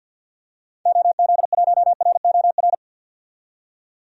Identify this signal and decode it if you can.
Morse code, heard as a keyed tone, about 50 wpm, 700 Hz. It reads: OZ1ROR
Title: DXCC-100 Call Signs - 6 Characters - 50wpm